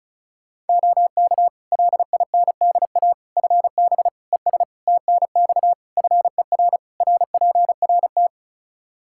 Morse code, heard as a keyed tone, about 34 wpm, 700 Hz. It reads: OK LINDA FB ES TNX FER RPRT
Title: Practice QSO - Exchange 2 - Info - 35wpm